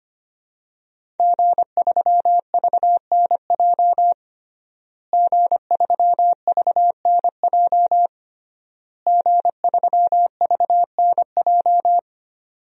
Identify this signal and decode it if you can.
Morse code, heard as a keyed tone, about 25 wpm, 700 Hz. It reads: G3VNJ G3VNJ G3VNJ
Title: DXCC-100 Call Signs - 5 Characters - Repeated 3x - 25wpm